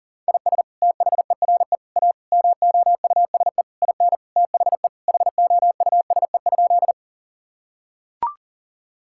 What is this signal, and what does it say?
Morse code, heard as a keyed tone, about 40 wpm, 700 Hz. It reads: IS THERE A MOUSE IN THE HOUSE?
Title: Sentences from Top 1000 Words - Review Part 5 - 40wpm